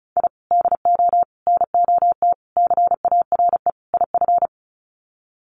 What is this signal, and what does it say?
Morse code, heard as a keyed tone, about 34 wpm, 700 Hz. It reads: I DO NOT CARE IF
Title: Sets of 5 Words 35wpm